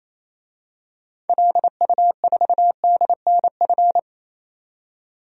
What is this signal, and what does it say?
Morse code, heard as a keyed tone, about 27 wpm, 700 Hz. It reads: LU4DNF
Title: DXCC-100 Call Signs - 6 Characters - 28wpm